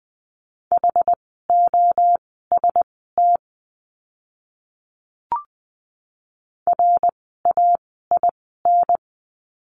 Morse code, heard as a keyed tone, about 20 wpm, spaced as for 14 wpm, 700 Hz. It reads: HOST RAIN
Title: 068 Lesson 12f - Letter d - Words 20wpm